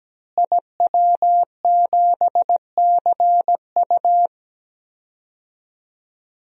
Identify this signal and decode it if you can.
Morse code, heard as a keyed tone, about 17 wpm, 700 Hz. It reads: IW7CU
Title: DXCC-100 Call Signs - 5 Characters - Repeated 2x - 17wpm